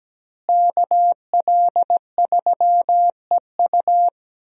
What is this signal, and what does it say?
Morse code, heard as a keyed tone, about 17 wpm, 700 Hz. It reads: KL3EU
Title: USA Call Signs - 5 Characters - Repeated 3x - 17wpm